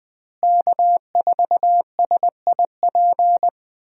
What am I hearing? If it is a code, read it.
Morse code, heard as a keyed tone, about 21 wpm, 700 Hz. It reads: K4SIP
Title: USA Call Signs - 5 Characters - Repeated 2x - 20wpm